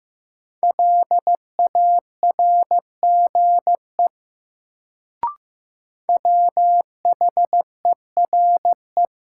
Morse code, heard as a keyed tone, about 15 wpm, 700 Hz. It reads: LARGE WHERE
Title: Sentences from Top 200 Words - New Word Drill - 15wpm